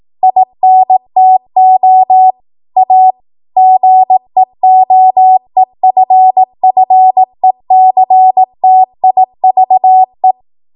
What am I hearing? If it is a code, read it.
Morse code, heard as a keyed tone, about 18 wpm, 750 Hz. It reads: INTO A GEOEFFECTIVE
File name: Morse / 18WPM